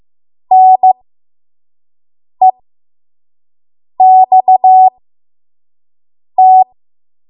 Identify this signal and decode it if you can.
Morse code, heard as a keyed tone, about 15 wpm, spaced as for 5 wpm, 750 Hz. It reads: NEXT
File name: Archive / 5 WPM / 05WPM